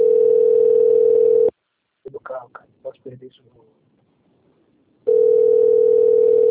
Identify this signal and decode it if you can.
Ringback tone (North America), 440 + 480 Hz, with speech.
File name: out- -2006- - - .24956